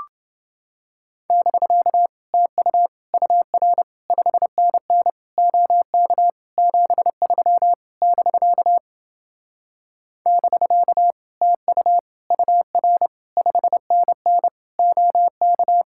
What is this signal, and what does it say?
Morse code, heard as a keyed tone, about 29 wpm, 700 Hz. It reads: <BK> TU UR 5NN OK 73 <BK> <BK> TU UR 5NN OK